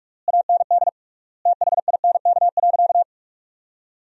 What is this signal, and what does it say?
Morse code, heard as a keyed tone, about 46 wpm, 700 Hz. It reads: AND THINK.
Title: 3x Word-Spacing - Sentences from Top 500 Words - 45wpm